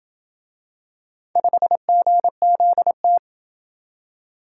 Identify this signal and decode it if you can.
Morse code, heard as a keyed tone, about 26 wpm, 700 Hz. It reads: 5GZT